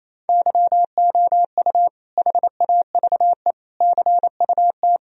Morse code, heard as a keyed tone, about 28 wpm, 700 Hz. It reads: YOU HAVE CUT
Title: Sentences from Top 1000 Words - Review Part 1 - 28wpm